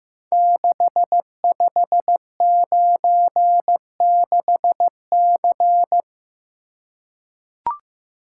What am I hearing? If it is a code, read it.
Morse code, heard as a keyed tone, about 15 wpm, 700 Hz. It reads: 6596C
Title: Code Groups - 5 Letter:Numbers - 15wpm